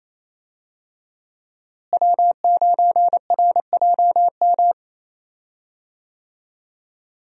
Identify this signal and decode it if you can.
Morse code, heard as a keyed tone, about 28 wpm, 700 Hz. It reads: W9RJM
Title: USA Call Signs - 5 Characters - Repeated 2x - 28wpm